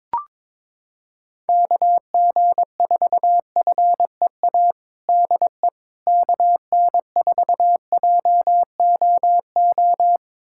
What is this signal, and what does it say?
Morse code, heard as a keyed tone, about 22 wpm, 700 Hz. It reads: KG4FEA DE KN4JOO
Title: USA Callsign DE Callsign - 6 Characters - 22wpm